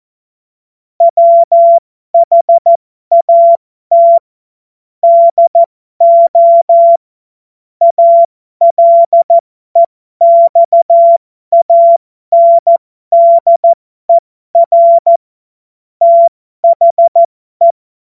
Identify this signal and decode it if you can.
Morse code, heard as a keyed tone, about 14 wpm, spaced as for 12 wpm, 680 Hz. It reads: WHAT DO ALEXANDER THE
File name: Morsekurs / Witze